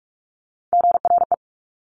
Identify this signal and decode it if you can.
Morse code, heard as a keyed tone, about 46 wpm, 700 Hz. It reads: GRE
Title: Top Three Letter Combinations in Words 45wpm